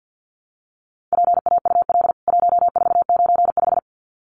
Morse code, heard as a keyed tone, about 49 wpm, 700 Hz. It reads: PAUL 1495
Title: CWT Exchange 50wpm